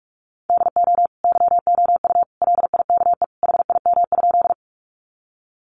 Morse code, heard as a keyed tone, about 44 wpm, 700 Hz. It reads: DO YOU LIKE HIM?
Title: Sentences from Top 100 Words 45wpm